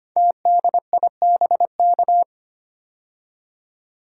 Morse code, heard as a keyed tone, about 25 wpm, 700 Hz. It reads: TDIBK
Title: Code Groups - 5 Letters - 25wpm